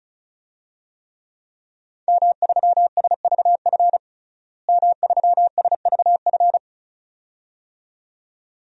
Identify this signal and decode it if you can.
Morse code, heard as a keyed tone, about 36 wpm, 700 Hz. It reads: M3SVF M3SVF
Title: DXCC-100 Call Signs - 5 Characters - Repeated 2x - 35wpm